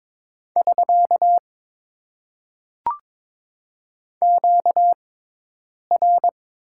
Morse code, heard as a keyed tone, about 22 wpm, spaced as for 8 wpm, 700 Hz. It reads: <SK> QR
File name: Top 90 QSO Elements - 6x Character Spacing - 22wpm